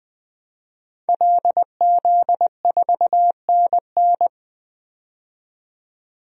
Morse code, heard as a keyed tone, about 20 wpm, 700 Hz. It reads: LZ4NN
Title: DXCC-100 Call Signs - 5 Characters - 20wpm